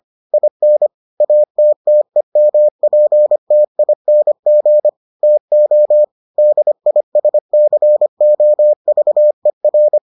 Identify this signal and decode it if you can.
Morse code, heard as a keyed tone, about 25 wpm, 600 Hz. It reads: IN ATTEMPTING TO DISCOVER